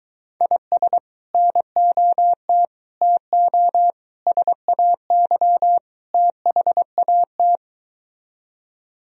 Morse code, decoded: IS NOT TO SAY THAT

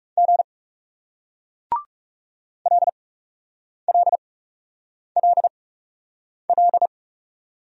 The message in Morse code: G LLLL

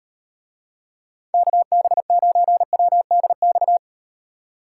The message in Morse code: KB9WDX

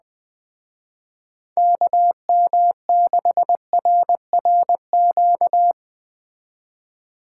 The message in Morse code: KM6RRQ

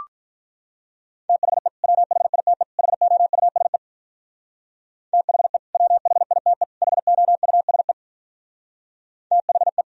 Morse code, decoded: THE WHITE HOUSE THE WHITE HOUSE THE

